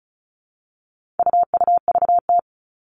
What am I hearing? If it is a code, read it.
Morse code, heard as a keyed tone, about 34 wpm, 700 Hz. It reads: UUVT
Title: All Cut Numbers - 4-Digit 35wpm